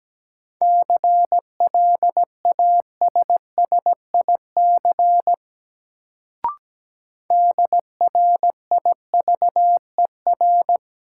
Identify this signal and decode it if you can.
Morse code, heard as a keyed tone, about 17 wpm, 700 Hz. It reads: CLASSIC DRIVER
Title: Sentences from Top 1000 Words - New Word Drill - 17wpm